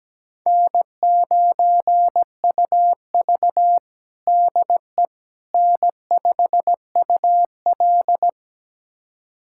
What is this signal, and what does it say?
Morse code, heard as a keyed tone, about 17 wpm, 700 Hz. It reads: N9UV DE N5UL
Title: USA Callsign DE Callsign - 4 Characters - 17wpm